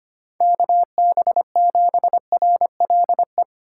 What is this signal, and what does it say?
Morse code, heard as a keyed tone, about 25 wpm, 700 Hz. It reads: KB7RLE
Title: USA Call Signs - 6 Characters - 2x Character Spacing - 25wpm